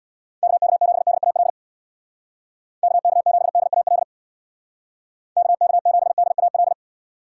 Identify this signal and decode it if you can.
Morse code, heard as a keyed tone, about 70 wpm, 700 Hz. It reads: KK7DAL KK7DAL KK7DAL